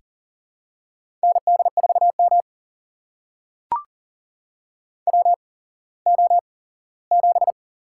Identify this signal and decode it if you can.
Morse code, heard as a keyed tone, about 42 wpm, spaced as for 20 wpm, 700 Hz. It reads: ND4M WO7
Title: USA Call Signs - 4 Characters - 8x Character Spacing - 40wpm